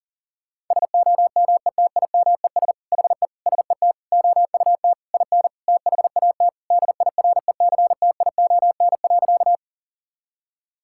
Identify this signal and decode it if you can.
Morse code, heard as a keyed tone, about 37 wpm, 700 Hz. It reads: SOMETIMES HE SET OUT IN THAT DIRECTION.
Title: Sentences from Top 1000 Words - Review Part 5 - 40wpm